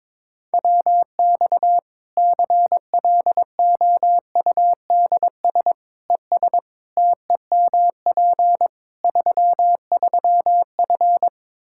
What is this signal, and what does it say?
Morse code, heard as a keyed tone, about 22 wpm, 700 Hz. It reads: WX CLOUDS ES TEMP 33F
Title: K1USN SST QSO Practice 22wpm